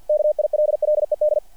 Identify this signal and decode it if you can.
Morse code, heard as a keyed tone, about 48 wpm, 600 Hz. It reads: KILLED
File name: archived / CRSnet